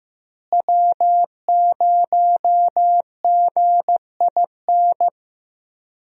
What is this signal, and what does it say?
Morse code, heard as a keyed tone, about 15 wpm, 700 Hz. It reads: W0GIN
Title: USA Call Signs - 5 Characters - Repeated 2x - 15wpm